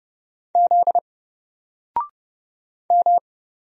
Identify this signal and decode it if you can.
Morse code, heard as a keyed tone, about 31 wpm, 700 Hz. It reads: Z M
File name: Single-Letter-Number-30wpm